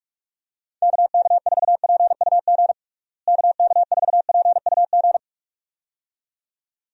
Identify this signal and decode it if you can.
Morse code, heard as a keyed tone, about 43 wpm, 700 Hz. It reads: KK4PUG KK4PUG
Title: USA Call Signs - 6 Characters - Repeated 2x - 45wpm